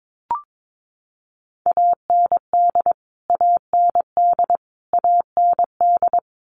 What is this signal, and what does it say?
Morse code, heard as a keyed tone, about 22 wpm, 700 Hz. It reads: AND AND AND
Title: Top 500 Words Repeated 22wpm